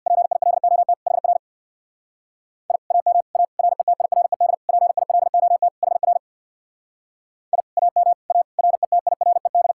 Morse code, heard as a keyed tone, about 65 wpm, 700 Hz. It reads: PILOT HR I AM A RETIRED PILOT HR I AM A RETIRED